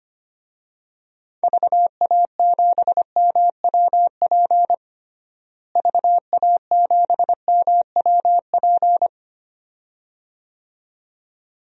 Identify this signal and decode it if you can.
Morse code, heard as a keyed tone, about 25 wpm, 700 Hz. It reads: VA7MWP VA7MWP
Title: DXCC-100 Call Signs - 6 Characters - Repeated 2x - 25wpm